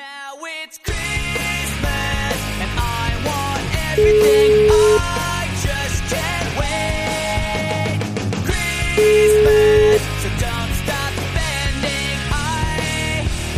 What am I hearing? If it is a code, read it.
Ringback tone (Europe, 425 Hz), with noise.